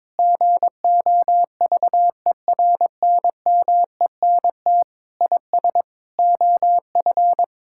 Morse code, heard as a keyed tone, about 22 wpm, 700 Hz. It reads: GOVERNMENT IS OF
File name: Sentences with 9-Word Phrases - 22wpm